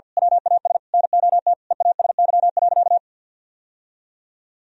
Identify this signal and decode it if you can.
Morse code, heard as a keyed tone, about 51 wpm, 700 Hz. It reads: WAS NOT EASY.
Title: Sentences from Top 1000 Words - Review Part 2 - 50wpm